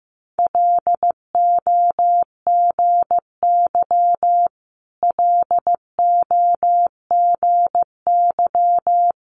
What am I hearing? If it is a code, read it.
Morse code, heard as a keyed tone, about 15 wpm, 700 Hz. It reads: LOGY LOGY
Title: Multi-letter Phonemes Repeated 3x 15wpm